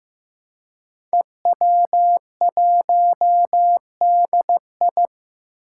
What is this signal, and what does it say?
Morse code, heard as a keyed tone, about 15 wpm, 700 Hz. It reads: EW1DI
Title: Five Challenging Letter-Number Combinations 15wpm